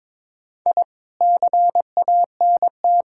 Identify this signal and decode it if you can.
Morse code, heard as a keyed tone, about 21 wpm, 700 Hz. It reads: I CANT
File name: Sentences with 7-Word Phrases - 22wpm